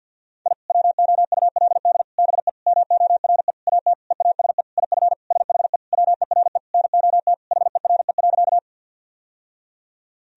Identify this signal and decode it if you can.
Morse code, heard as a keyed tone, about 51 wpm, 700 Hz. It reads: I WOULD BE MORE AT EASE IF SHE WERE NOT HERE.
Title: Sentences from Top 1000 Words - Review Part 1 - 50wpm